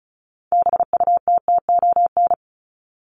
Morse code, heard as a keyed tone, about 34 wpm, 700 Hz. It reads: BUTTON
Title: Top 1001-1250 Words 35wpm